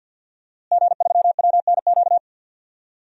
Morse code, heard as a keyed tone, about 46 wpm, 700 Hz. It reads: G3WNQ
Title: DXCC-100 Call Signs - 5 Characters - Repeated 3x - 50wpm